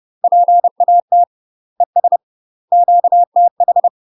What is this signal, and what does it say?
Morse code, heard as a keyed tone, about 30 wpm, 700 Hz. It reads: PAT ES QTH
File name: UK MASTER QSOs TRAINING / 30WPM / UK MASTER QSOs 0032